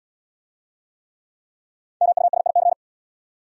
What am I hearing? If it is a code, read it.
Morse code, heard as a keyed tone, about 70 wpm, 690 Hz. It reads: MUSIC